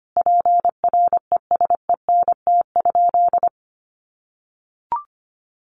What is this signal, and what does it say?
Morse code, heard as a keyed tone, about 25 wpm, 700 Hz. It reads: PRESENT?
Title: Sentences from Top 800 Words 25wpm